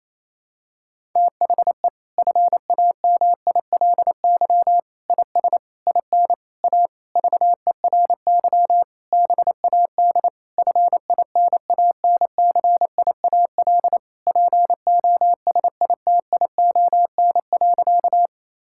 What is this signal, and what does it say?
Morse code, heard as a keyed tone, about 28 wpm, 700 Hz. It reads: THE FAMILY IS IN A VERY BAD FINANCIAL POSITION.
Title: Sentences from Top 1000 Words - Review Part 5 - 28wpm